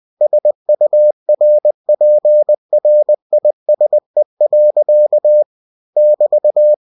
Morse code, heard as a keyed tone, about 20 wpm, 600 Hz. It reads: SURPRISE. <BT>
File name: cw-books / winnie-the-pooh / winnie-20wpm-page-0082